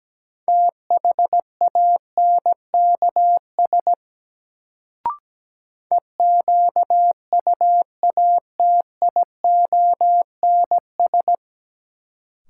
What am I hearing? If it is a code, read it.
Morse code, heard as a keyed tone, about 17 wpm, 700 Hz. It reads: THANKS EQUATIONS
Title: Sets of 5 Words - New Word Drill - 17wpm